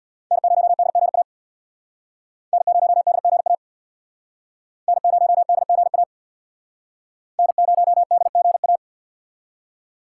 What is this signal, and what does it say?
Morse code, heard as a keyed tone, about 66 wpm, 700 Hz. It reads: N0DGA N0DGA N0DGA N0DGA